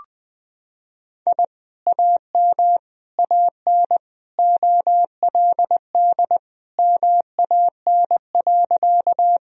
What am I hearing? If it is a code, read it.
Morse code, heard as a keyed tone, about 21 wpm, 700 Hz. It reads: I AM AN OLD MAN.
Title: Sentences from Top 1000 Words - Review Part 2 - 20wpm